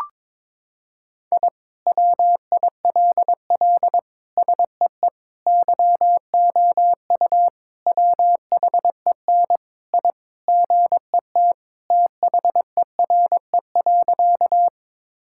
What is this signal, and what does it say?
Morse code, heard as a keyed tone, about 22 wpm, 700 Hz. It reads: I WILL SEE YOU WHEN I GET THERE.
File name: practice / 22wpm / Sentences with 2-Word Phrases - 22wpm